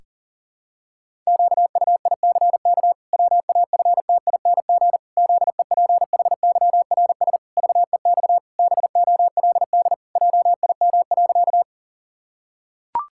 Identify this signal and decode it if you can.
Morse code, heard as a keyed tone, about 37 wpm, 700 Hz. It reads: QUICK WAFTING ZEPHYRS VEX BOLD JIM.
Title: Pangram Challenge v2 - 40wpm